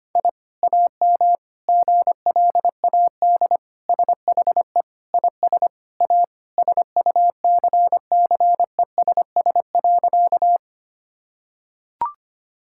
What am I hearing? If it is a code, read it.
Morse code, heard as a keyed tone, about 25 wpm, 700 Hz. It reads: I AM GLAD SHE IS A SUCCESS.
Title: Sentences from Top 1000 Words - Review Part 6 - 25wpm